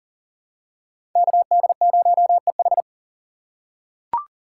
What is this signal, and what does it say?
Morse code, heard as a keyed tone, about 37 wpm, 700 Hz. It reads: KD0EH
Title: USA Call Signs - 5 Characters - 6x Character Spacing - 40wpm